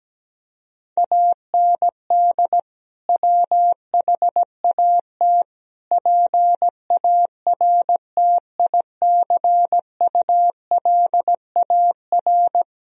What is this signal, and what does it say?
Morse code, heard as a keyed tone, about 17 wpm, 700 Hz. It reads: AND WHAT PARTICULAR
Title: Sentences from Top 1000 Words - Review Part 4 - 17wpm